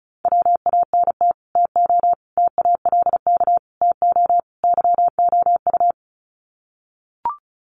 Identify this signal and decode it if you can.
Morse code, heard as a keyed tone, about 34 wpm, 700 Hz. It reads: WANT TO TALK TO YOU
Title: Sets of 5 Words Encore 35wpm